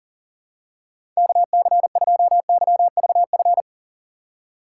Morse code, heard as a keyed tone, about 40 wpm, 700 Hz. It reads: KC2YVF
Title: USA Call Signs - 3x Character Spacing - 40wpm